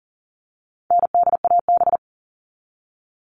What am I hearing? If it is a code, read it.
Morse code, heard as a keyed tone, about 40 wpm, 700 Hz. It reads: NDAB
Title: All Cut Numbers - 4-Digit 40wpm